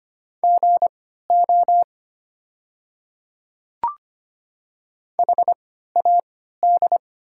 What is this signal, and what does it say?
Morse code, heard as a keyed tone, about 25 wpm, spaced as for 14 wpm, 700 Hz. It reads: GO HAD
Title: 3x Character Spacing - Top 100 Words - 25wpm